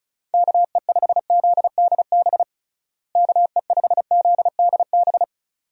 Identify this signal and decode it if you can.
Morse code, heard as a keyed tone, about 34 wpm, 700 Hz. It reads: KE5ZDB KE5ZDB
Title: USA Call Signs - 6 Characters - Repeated 2x - 35wpm